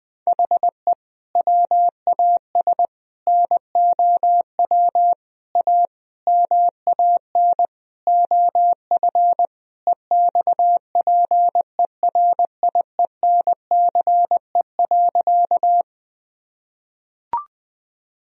Morse code, heard as a keyed tone, about 20 wpm, 700 Hz. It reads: HE WAS NOW A MAN OF EXPERIENCE.
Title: Sentences from Top 1000 Words - Review Part 3 - 20wpm